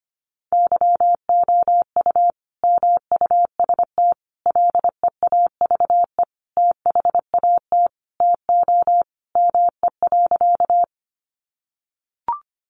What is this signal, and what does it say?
Morse code, heard as a keyed tone, about 25 wpm, 700 Hz. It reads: YOU MUST LEAVE THAT TO ME.
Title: Sentences from Top 400 Words 25wpm